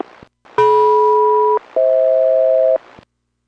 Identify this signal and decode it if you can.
Aircraft SELCAL, heard as two pairs of tones. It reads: DM-FH